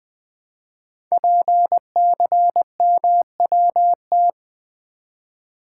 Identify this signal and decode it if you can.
Morse code, heard as a keyed tone, about 20 wpm, 700 Hz. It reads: PCMWT
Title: Code Groups - 5 Letters - 20wpm